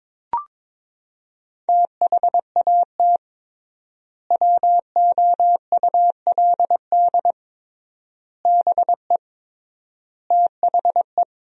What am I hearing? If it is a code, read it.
Morse code, heard as a keyed tone, about 22 wpm, 700 Hz. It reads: THAT WOULD BE THE
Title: 3x Word-Spacing - Sentences from Top 400 Words - 22wpm